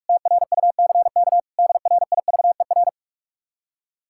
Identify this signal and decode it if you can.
Morse code, heard as a keyed tone, about 67 wpm, spaced as for 55 wpm, 700 Hz. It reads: TRUCK DRIVER
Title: Speed-Racing Common QSO Phrases - 45wpm